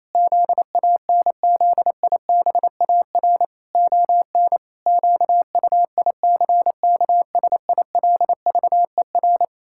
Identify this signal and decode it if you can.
Morse code, heard as a keyed tone, about 28 wpm, 700 Hz. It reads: ZANZIBAR ON QUICKSILVER
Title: Pangram Challenge v2 - 28wpm